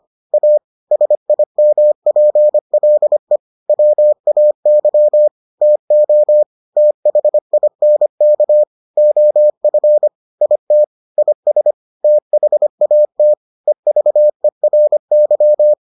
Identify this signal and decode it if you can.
Morse code, heard as a keyed tone, about 25 wpm, 600 Hz. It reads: A SIMPLE WAY TO THINK OF IT IS THAT EVERY